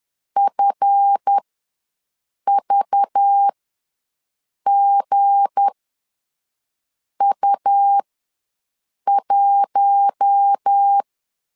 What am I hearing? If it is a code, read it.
Morse code, heard as a keyed tone, about 10 wpm, spaced as for 5 wpm, 800 Hz. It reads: FVGU1
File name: LEZIONI / 7 GF9Z / 7 GF ws 10wpm 3918